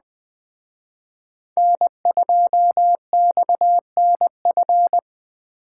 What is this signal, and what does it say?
Morse code, heard as a keyed tone, about 20 wpm, 700 Hz. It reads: N2XNF
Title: Speed Racing - USA 1x3 Call Signs 17wpm